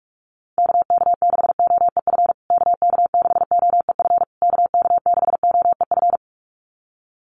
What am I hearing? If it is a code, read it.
Morse code, heard as a keyed tone, about 44 wpm, 690 Hz. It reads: KK6OEF KK6OEF KK6OEF
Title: USA Call Signs Repeated 3x 45wpm